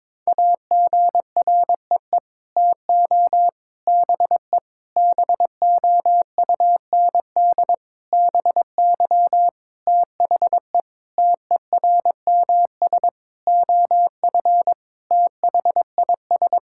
Morse code, heard as a keyed tone, about 22 wpm, 700 Hz. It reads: AGREE TO BE BOUND BY THE TERMS OF THIS
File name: Sets of 9 Words - 22wpm